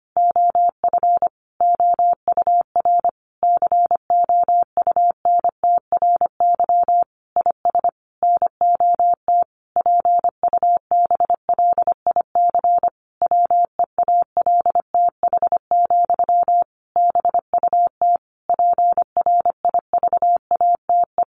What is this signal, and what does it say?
Morse code, heard as a keyed tone, about 25 wpm, 700 Hz. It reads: OF OUR COUNTRY IS NOT PUBLIC WEALTH, BUT PRIVATE
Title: Sentences from Top 1200 Words 25wpm